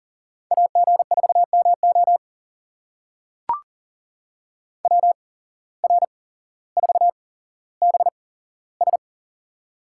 Morse code, heard as a keyed tone, about 43 wpm, spaced as for 17 wpm, 700 Hz. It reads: AG4MO WR4BS
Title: USA Call Signs - 5 Characters - 8x Character Spacing - 40wpm